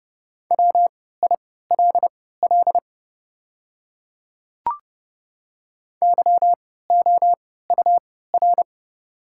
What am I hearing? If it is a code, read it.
Morse code, heard as a keyed tone, about 30 wpm, spaced as for 17 wpm, 690 Hz. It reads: WILL YOUR
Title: 3x Character Spacing - Top 200 Words - 30wpm